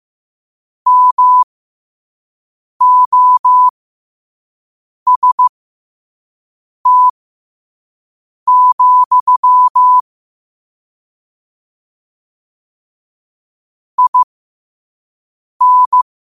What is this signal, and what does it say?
Morse code, heard as a keyed tone, about 15 wpm, spaced as for 5 wpm, 1000 Hz. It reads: MOST, IN
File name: qotdfiles / 5wpm